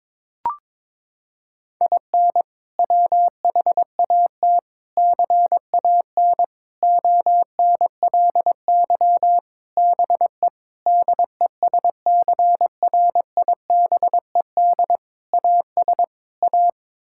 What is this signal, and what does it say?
Morse code, heard as a keyed tone, about 22 wpm, 700 Hz. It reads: IN WHAT CAN ONLY BE DESCRIBED AS A
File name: practice / 22wpm / Sets of 8 Words - 22wpm